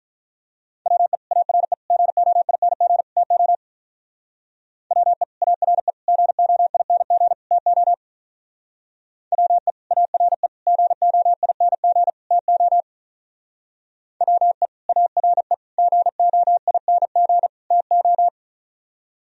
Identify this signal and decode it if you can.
Morse code, heard as a keyed tone, about 42 wpm, 700 Hz. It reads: WE ARE GOING TO WE ARE GOING TO WE ARE GOING TO WE ARE GOING TO
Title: Speed-Racing - Sets of 4 Words 35wpm